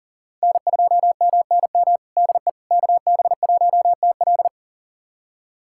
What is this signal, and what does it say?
Morse code, heard as a keyed tone, about 40 wpm, 700 Hz. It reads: N2MNM DE KB1TL
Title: USA Callsign DE Callsign - 5 Characters - 40wpm